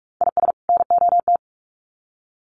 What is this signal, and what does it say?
Morse code, heard as a keyed tone, about 44 wpm, 700 Hz. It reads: IS NOT
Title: Sets of 2 Words 45wpm